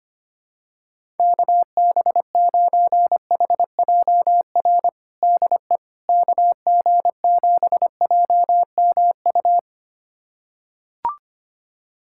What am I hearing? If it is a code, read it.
Morse code, heard as a keyed tone, about 25 wpm, 700 Hz. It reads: KB9HJR DE KG7JMU
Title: USA Callsign DE Callsign - 6 Characters - 25wpm